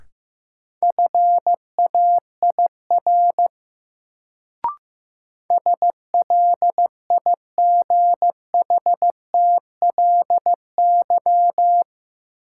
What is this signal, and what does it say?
Morse code, heard as a keyed tone, about 15 wpm, 700 Hz. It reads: FAIR SLIGHTLY
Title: Sentences from Top 1300 Words - New Word Drill Alt - 15wpm